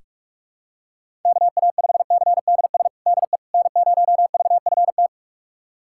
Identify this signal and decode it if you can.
Morse code, heard as a keyed tone, about 46 wpm, 700 Hz. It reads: KA5CDS DE N0VFT
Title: USA Callsign DE Callsign 45wpm